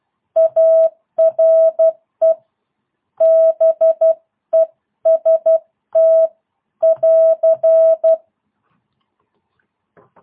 Morse code, heard as a keyed tone, about 12 wpm, 650 Hz. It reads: ARE BEST <AR>
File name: jh8cbh / 2022cw